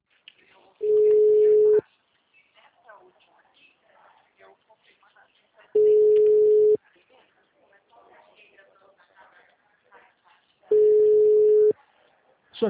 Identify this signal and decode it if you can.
Ringback tone (Europe, 425 Hz), with speech.